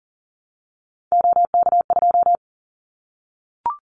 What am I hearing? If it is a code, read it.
Morse code, heard as a keyed tone, about 39 wpm, 700 Hz. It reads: OK2
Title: Callsign Prefixes for Top 25 DXCC - Repeated 3x - 40wpm